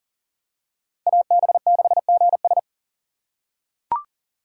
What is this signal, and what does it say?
Morse code, heard as a keyed tone, about 40 wpm, 700 Hz. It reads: AB6GS
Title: USA Call Signs - 5 Characters - 6x Character Spacing - 40wpm